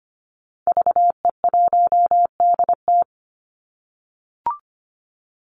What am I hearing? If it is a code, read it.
Morse code, heard as a keyed tone, about 25 wpm, 700 Hz. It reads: VE1DT
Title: Call Signs - 25wpm